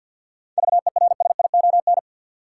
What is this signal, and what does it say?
Morse code, heard as a keyed tone, about 53 wpm, 700 Hz. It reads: VERSION
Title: Sentences from Top 1400 Words - New Word Drill - 50wpm